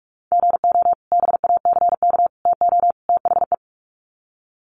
Morse code, heard as a keyed tone, about 44 wpm, 700 Hz. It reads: GO BACK TO THE
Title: Sets of 4 Words 45wpm